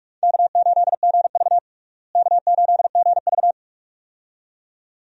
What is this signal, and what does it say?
Morse code, heard as a keyed tone, about 46 wpm, 700 Hz. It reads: K8GV K8GV
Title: USA Call Signs - 4 Characters - Repeated 2x - 45wpm